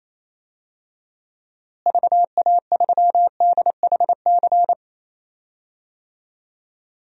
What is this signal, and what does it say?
Morse code, heard as a keyed tone, about 28 wpm, 700 Hz. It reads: VA3DHC
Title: DXCC-100 Call Signs - 6 Characters - Repeated 2x - 28wpm